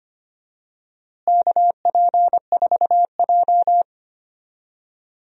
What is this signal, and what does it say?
Morse code, heard as a keyed tone, about 25 wpm, 700 Hz. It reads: KP4J
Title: DXCC-100 Call Signs - 4 Characters - Repeated 2x - 25wpm